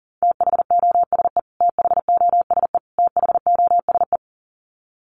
Morse code, heard as a keyed tone, about 40 wpm, 700 Hz. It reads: THOSE THOSE THOSE
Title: Top 300 Words Repeated 40wpm